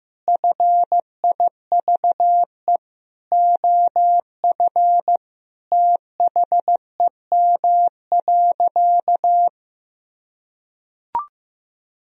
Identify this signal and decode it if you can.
Morse code, heard as a keyed tone, about 16 wpm, 700 Hz. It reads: FIVE OF THEM.
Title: Sentences from Top 1000 Words - Review Part 2 - 15wpm